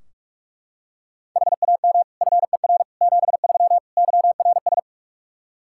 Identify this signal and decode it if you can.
Morse code, heard as a keyed tone, about 42 wpm, 700 Hz. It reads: HAM FER 73 YRS
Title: Practice QSO - Exchange 3 - More Info - 45wpm